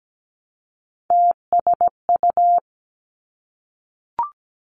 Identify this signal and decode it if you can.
Morse code, heard as a keyed tone, about 17 wpm, 700 Hz. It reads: TSU
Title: Three Challenging Letter-Number Combinations 17wpm